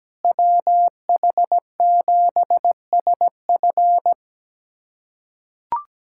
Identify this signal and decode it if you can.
Morse code, heard as a keyed tone, about 17 wpm, 700 Hz. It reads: WH7SF
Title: USA Callsign DE Callsign - 5 Characters - 17wpm